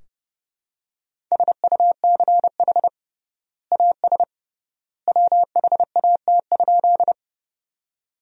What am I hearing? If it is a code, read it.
Morse code, heard as a keyed tone, about 32 wpm, 700 Hz. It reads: SUCH AS WHAT?
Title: 3x Word-Spacing - Sentences from Top 200 Words - 30wpm